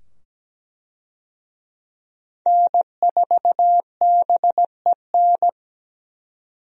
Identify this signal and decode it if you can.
Morse code, heard as a keyed tone, about 17 wpm, 700 Hz. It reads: N4BEN
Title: USA Call Signs - 5 Characters - 17wpm